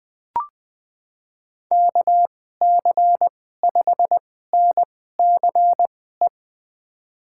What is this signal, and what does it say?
Morse code, heard as a keyed tone, about 20 wpm, spaced as for 15 wpm, 700 Hz. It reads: KC5NCE
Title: USA Call Signs - 6 Characters - 2x Character Spacing - 20wpm